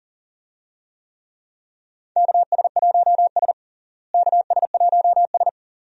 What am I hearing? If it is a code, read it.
Morse code, heard as a keyed tone, about 40 wpm, 700 Hz. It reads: KS1S KS1S
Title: USA Call Signs - 4 Characters - Repeated 2x - 40wpm